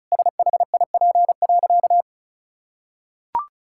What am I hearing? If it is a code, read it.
Morse code, heard as a keyed tone, about 36 wpm, 700 Hz. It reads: SHIP.
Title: Sentences from Top 1000 Words - Review Part 5 - 35wpm